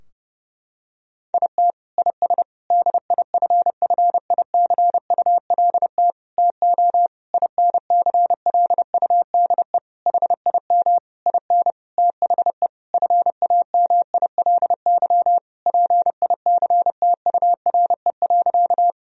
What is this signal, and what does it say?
Morse code, heard as a keyed tone, about 28 wpm, 700 Hz. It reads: IT IS DIFFICULT TO INCLUDE HIM IN THE FAMILY PICTURE.
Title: Sentences from Top 1000 Words - Review Part 3 - 30wpm